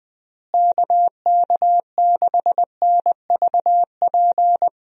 Morse code, heard as a keyed tone, about 20 wpm, 700 Hz. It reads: KK6NVP
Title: USA Call Signs - 6 Characters - Repeated 3x - 20wpm